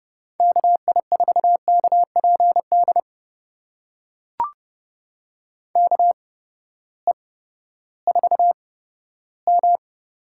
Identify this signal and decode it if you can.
Morse code, heard as a keyed tone, about 30 wpm, spaced as for 15 wpm, 700 Hz. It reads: KI4KPD KE4M
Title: USA Call Signs - 8x Character Spacing - 30wpm